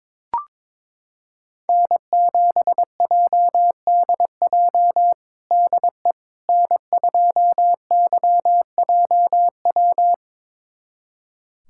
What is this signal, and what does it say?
Morse code, heard as a keyed tone, about 22 wpm, 700 Hz. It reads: N7JDJ DE N2YJW
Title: USA Callsign DE Callsign 22wpm